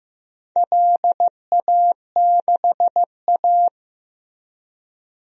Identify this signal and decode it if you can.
Morse code, heard as a keyed tone, about 15 wpm, 700 Hz. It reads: LA6A
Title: DXCC-100 Call Signs - 4 Characters - Repeated 3x - 15wpm